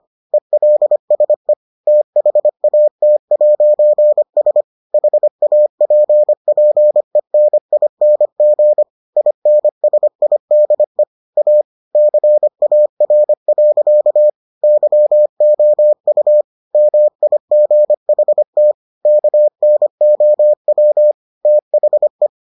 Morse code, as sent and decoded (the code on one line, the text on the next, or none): . .-.. ... / - .... .- - .----. ... / .... .- .--. .--. -. .. -. --. / .. -. ... .. -.. / .- / -.-. .- .-. .-.-.- / -.-- --- ..- / -- .. --. .... - / -.- -. --- .-- / - .... .
ELSE THAT'S HAPPENING INSIDE A CAR. YOU MIGHT KNOW THE